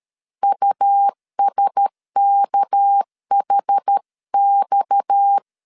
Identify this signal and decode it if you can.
Morse code, heard as a keyed tone, about 13 wpm, 800 Hz. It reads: USKHX
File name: CW / DET / 1 6 1500C S A 12wpm 3440